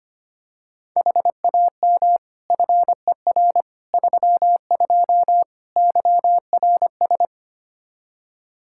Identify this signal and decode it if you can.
Morse code, heard as a keyed tone, about 25 wpm, 700 Hz. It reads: HAM FER 32 YRS
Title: Practice QSO - Exchange 3 - More Info - 25wpm